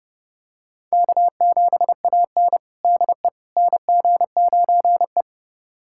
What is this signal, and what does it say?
Morse code, heard as a keyed tone, about 30 wpm, 700 Hz. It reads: K7AN DE NG9E
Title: USA Callsign DE Callsign - 4 Characters - 30wpm